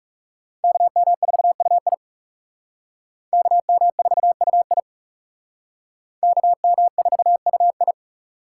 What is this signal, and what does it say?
Morse code, heard as a keyed tone, about 37 wpm, 700 Hz. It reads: KM4UI KM4UI KM4UI
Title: Speed Racing - USA 2x2 Call Signs 30wpm